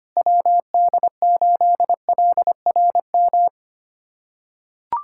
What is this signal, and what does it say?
Morse code, heard as a keyed tone, about 25 wpm, 700 Hz. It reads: WD8LRM
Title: USA Callsign DE Callsign - 6 Characters - 25wpm